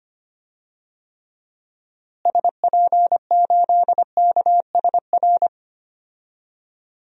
Morse code, heard as a keyed tone, about 25 wpm, 690 Hz. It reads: SP8KSR
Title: DXCC-100 Call Signs - 6 Characters - Repeated 3x - 25wpm